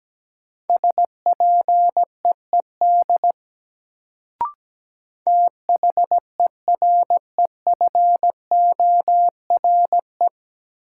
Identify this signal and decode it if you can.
Morse code, heard as a keyed tone, about 17 wpm, 700 Hz. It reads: SPEED THEREFORE
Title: Sentences from Top 800 Words - New Word Drill - 17wpm